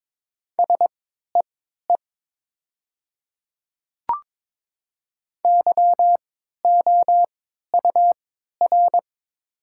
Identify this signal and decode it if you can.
Morse code, heard as a keyed tone, about 22 wpm, spaced as for 12 wpm, 700 Hz. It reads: SEE YOUR
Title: 3x Character Spacing - Top 100 Words - 22wpm